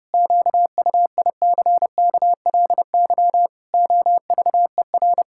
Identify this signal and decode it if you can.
Morse code, heard as a keyed tone, about 32 wpm, 700 Hz. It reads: QUICKLY OVER
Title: Pangram Challenge v2 - 30wpm